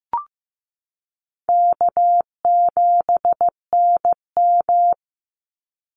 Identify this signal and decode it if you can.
Morse code, heard as a keyed tone, about 15 wpm, 700 Hz. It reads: K7NM
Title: Easy Call Signs Repeated 3x - 15wpm